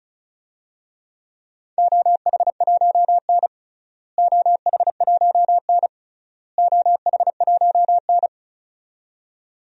Morse code, heard as a keyed tone, about 36 wpm, 700 Hz. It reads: OH1N OH1N OH1N
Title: DXCC-100 Call Signs - 4 Characters - Repeated 3x - 35wpm